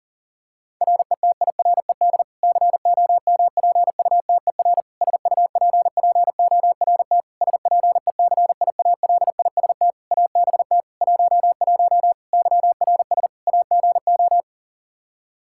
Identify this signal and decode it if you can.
Morse code, heard as a keyed tone, about 40 wpm, 700 Hz. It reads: RETIRED COMPUTER SUPPORT SPECIALIST ABT 11 YRS AGO
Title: Practice QSO - Exchange 3 - More Info - 40wpm